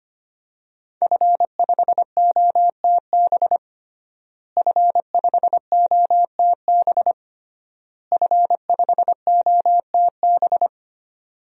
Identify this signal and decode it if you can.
Morse code, heard as a keyed tone, about 25 wpm, 700 Hz. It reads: F5OTB F5OTB F5OTB
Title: DXCC-100 Call Signs - 5 Characters - Repeated 3x - 25wpm